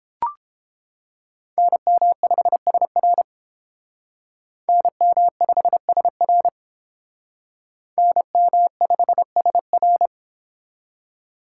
Morse code, heard as a keyed tone, about 29 wpm, 700 Hz. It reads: NM5SR NM5SR NM5SR